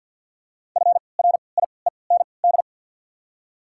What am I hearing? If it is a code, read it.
Morse code, heard as a keyed tone, about 53 wpm, spaced as for 29 wpm, 690 Hz. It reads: FRIEND
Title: 3x Character Spacing - Top 500 Words - 50wpm